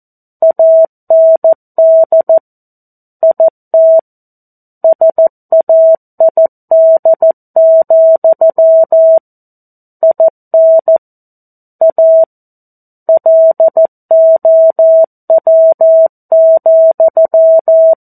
Morse code, heard as a keyed tone, about 14 wpm, 650 Hz. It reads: AND IT SAID, IN A LOW,
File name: 02-Chapter-02 14 WPM